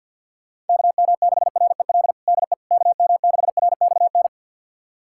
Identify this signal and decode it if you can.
Morse code, heard as a keyed tone, about 51 wpm, 700 Hz. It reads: KM6REL DE KM6RXN
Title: USA Callsign DE Callsign - 6 Characters - 50wpm